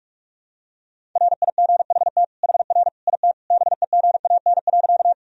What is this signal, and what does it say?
Morse code, heard as a keyed tone, about 41 wpm, 700 Hz. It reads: RIGHT HR IT BEGAN.
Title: Sentences from Top 300 Words and Abbreviations 45wpm